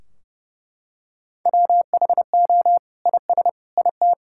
Morse code, heard as a keyed tone, about 32 wpm, 700 Hz. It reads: WHO IS IT
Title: Sentences from Top 1600 Words 30wpm